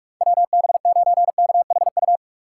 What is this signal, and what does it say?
Morse code, heard as a keyed tone, about 46 wpm, 700 Hz. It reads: WB9KHU
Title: USA Call Signs - 3x Character Spacing - 45wpm